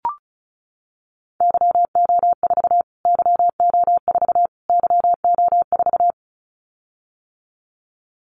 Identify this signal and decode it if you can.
Morse code, heard as a keyed tone, about 34 wpm, 700 Hz. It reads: YO4 YO4 YO4
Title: Callsign Prefixes for Top 25 DXCC - Repeated 3x - 35wpm